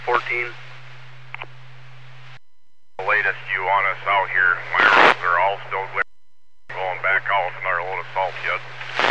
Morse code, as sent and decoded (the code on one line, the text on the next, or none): none